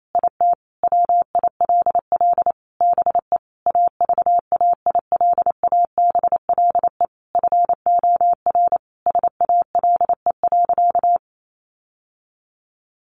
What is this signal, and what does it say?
Morse code, heard as a keyed tone, about 28 wpm, 700 Hz. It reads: IT WILL BE AVAILABLE FOR SALE.
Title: Sentences from Top 1400 Words 28wpm